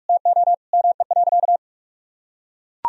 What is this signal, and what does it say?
Morse code, heard as a keyed tone, about 43 wpm, 700 Hz. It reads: TO ME.
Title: Sentences from Top 200 Words and Abbreviations 45wpm